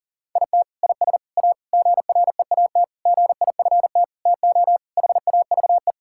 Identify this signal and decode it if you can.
Morse code, heard as a keyed tone, about 40 wpm, 700 Hz. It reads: IT IS A GREAT GIFT TO HAVE
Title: Sentences from Top 1000 Words - Review Part 6 - 40wpm